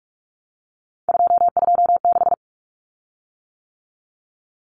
Morse code, heard as a keyed tone, about 44 wpm, 700 Hz. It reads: 226
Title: Common Cut Numbers - 3-Digit 45wpm